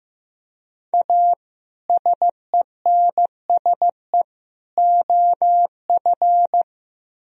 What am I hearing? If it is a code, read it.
Morse code, heard as a keyed tone, about 18 wpm, 700 Hz. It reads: A SENSE OF
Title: Speed-Racing - Sets of 3 Words 15wpm